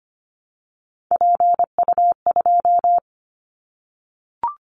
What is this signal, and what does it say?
Morse code, heard as a keyed tone, about 25 wpm, 700 Hz. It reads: PU2
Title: Callsign Prefixes for Top 100 DXCC - 25wpm